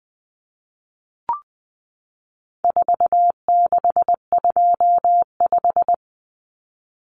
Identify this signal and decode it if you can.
Morse code, heard as a keyed tone, about 20 wpm, 700 Hz. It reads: 4625
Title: Common Cut Numbers - 4-Digit 20wpm